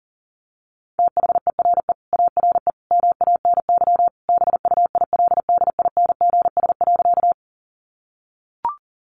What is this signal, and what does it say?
Morse code, heard as a keyed tone, about 40 wpm, 700 Hz. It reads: THERE ARE MANY BUILDINGS.
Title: Easy Phrases 40wpm